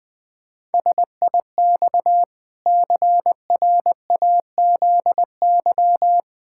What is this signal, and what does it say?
Morse code, heard as a keyed tone, about 20 wpm, 700 Hz. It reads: SIX CRAZY
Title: Pangram Challenge v2 - 20wpm